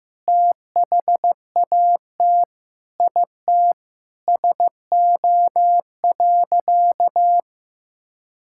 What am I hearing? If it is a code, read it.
Morse code, heard as a keyed tone, about 15 wpm, 700 Hz. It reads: THAT IT SO.
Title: Sentences from Top 1600 Words 15wpm